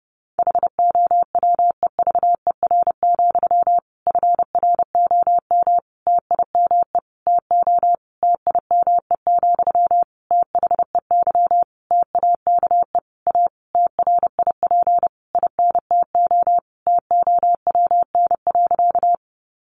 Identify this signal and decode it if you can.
Morse code, heard as a keyed tone, about 30 wpm, 700 Hz. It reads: HOWEVER, FROM TIME TO TIME, THEY TAKE A TRIP INTO TOWN.
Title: Sentences from Top 1300 Words 30wpm